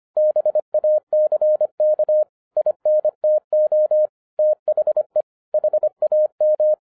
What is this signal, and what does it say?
Morse code, heard as a keyed tone, about 25 wpm, 610 Hz. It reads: BACK INTO THE HAM